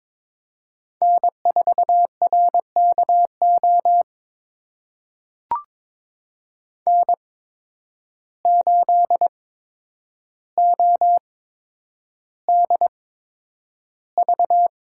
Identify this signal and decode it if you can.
Morse code, heard as a keyed tone, about 22 wpm, spaced as for 9 wpm, 700 Hz. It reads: N4RKO N8ODV
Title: USA Call Signs - 5 Characters - 8x Character Spacing - 22wpm